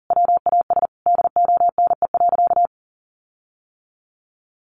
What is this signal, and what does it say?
Morse code, heard as a keyed tone, about 40 wpm, 700 Hz. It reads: WAS DONE.
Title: Sentences from Top 1500 Words 40wpm